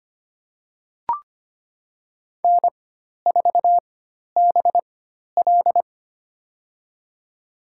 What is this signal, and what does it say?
Morse code, heard as a keyed tone, about 25 wpm, spaced as for 12 wpm, 690 Hz. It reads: N4BL
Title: USA Call Signs - 4 Characters - 4x Character Spacing - 25wpm